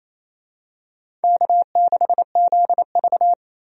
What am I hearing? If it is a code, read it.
Morse code, heard as a keyed tone, about 29 wpm, 690 Hz. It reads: K6ZV
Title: USA Call Signs - 4 Characters - 28wpm